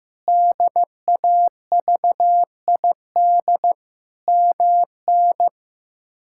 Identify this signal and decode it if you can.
Morse code, heard as a keyed tone, about 15 wpm, 700 Hz. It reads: DAVID MN
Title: K1USN SST QSO Practice - 15wpm